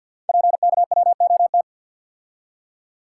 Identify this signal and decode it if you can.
Morse code, heard as a keyed tone, about 51 wpm, 700 Hz. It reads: PKWOT
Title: Code Groups - 5 Letters - 50wpm